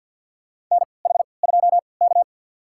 Morse code, heard as a keyed tone, about 53 wpm, spaced as for 29 wpm, 700 Hz. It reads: NH2K
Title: USA Call Signs - 4 Characters - 3x Character Spacing - 50wpm